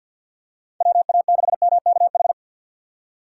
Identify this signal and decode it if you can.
Morse code, heard as a keyed tone, about 53 wpm, 700 Hz. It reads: WA6MKH
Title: USA Call Signs - 6 Characters - 3x Character Spacing - 50wpm